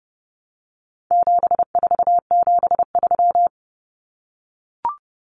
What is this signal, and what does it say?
Morse code, heard as a keyed tone, about 30 wpm, 700 Hz. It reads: 7473
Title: Common Cut Numbers - 4-Digit 30wpm